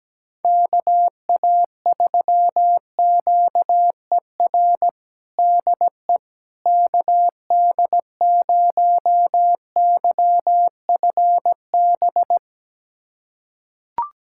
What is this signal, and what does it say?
Morse code, heard as a keyed tone, about 17 wpm, 700 Hz. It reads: KA3QER DE KD0YFB
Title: USA Callsign DE Callsign - 6 Characters - 17wpm